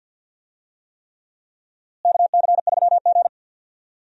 Morse code, heard as a keyed tone, about 46 wpm, 700 Hz. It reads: KC3G
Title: USA Call Signs - 4 Characters - Repeated 3x - 50wpm